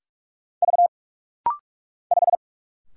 Morse code, heard as a keyed tone, about 46 wpm, 700 Hz. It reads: V 5
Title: Single Letter-Number - Rapid-Fire 45wpm